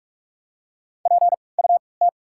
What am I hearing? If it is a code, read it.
Morse code, heard as a keyed tone, about 48 wpm, spaced as for 26 wpm, 700 Hz. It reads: PUT
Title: 3x Character Spacing - Top 300 Words - 45wpm